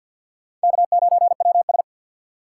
Morse code, heard as a keyed tone, about 48 wpm, 700 Hz. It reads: K9WS
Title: USA Call Signs - 2x Character Spacing - 50wpm